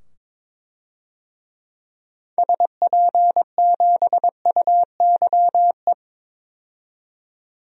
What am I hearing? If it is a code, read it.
Morse code, heard as a keyed tone, about 22 wpm, 700 Hz. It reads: SP7UYE
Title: DXCC-100 Call Signs - 6 Characters - 22wpm